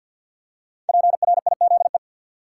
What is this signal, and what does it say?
Morse code, heard as a keyed tone, about 46 wpm, 700 Hz. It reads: PRIZE
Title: Five Letter Words - 50wpm